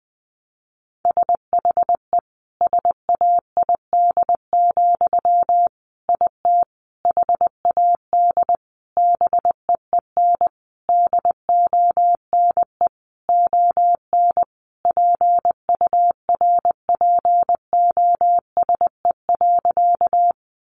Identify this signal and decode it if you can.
Morse code, heard as a keyed tone, about 20 wpm, 700 Hz. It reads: SHE SAID, IT HAD BEEN DONE ON PURPOSE.
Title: Sentences from Top 800 Words 20wpm